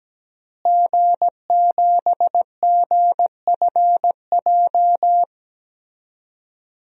Morse code, heard as a keyed tone, about 17 wpm, 700 Hz. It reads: G7GFJ
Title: DXCC-100 Call Signs - 5 Characters - Repeated 3x - 17wpm